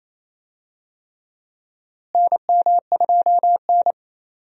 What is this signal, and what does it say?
Morse code, heard as a keyed tone, about 28 wpm, 700 Hz. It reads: NM2N
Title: USA Call Signs - 4 Characters - Repeated 3x - 28wpm